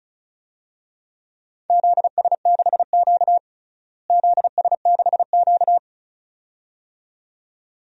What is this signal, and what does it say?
Morse code, heard as a keyed tone, about 34 wpm, 700 Hz. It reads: ZS6Q ZS6Q
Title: DXCC-100 Call Signs - 4 Characters - Repeated 2x - 35wpm